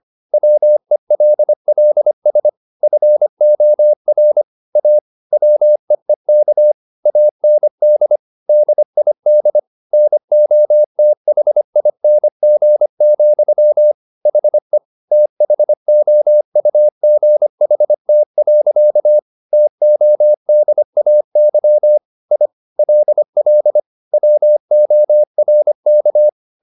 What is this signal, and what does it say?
Morse code, heard as a keyed tone, about 25 wpm, 600 Hz. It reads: WELLS FOR A WEEK AND DID NOTHING, HE THOUGHT. TODAY I LL WORK
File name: the old man and the sea part 1 25wpm